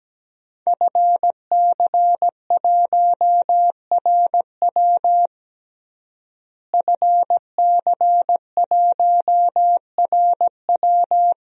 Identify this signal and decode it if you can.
Morse code, heard as a keyed tone, about 17 wpm, 700 Hz. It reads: FC1RW FC1RW
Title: DXCC-100 Call Signs - 5 Characters - Repeated 3x - 17wpm